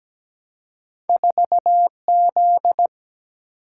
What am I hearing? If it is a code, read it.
Morse code, heard as a keyed tone, about 17 wpm, 700 Hz. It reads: 4Z
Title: Single-Letter Single-Number 17wpm